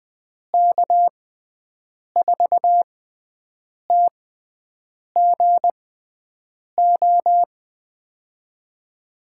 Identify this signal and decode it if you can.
Morse code, heard as a keyed tone, about 20 wpm, spaced as for 7 wpm, 700 Hz. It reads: K4TGO
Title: USA Call Signs - 6 Characters - 6x Character Spacing - 20wpm